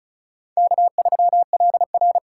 Morse code, heard as a keyed tone, about 36 wpm, 700 Hz. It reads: K3LR
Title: K1USN SST QSO Practice - 35wpm